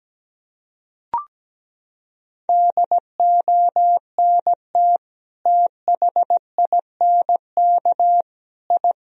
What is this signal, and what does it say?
Morse code, heard as a keyed tone, about 17 wpm, 700 Hz. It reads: DONT THINK I
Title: Sentences from Top 1000 Words - Review Part 3 - 17wpm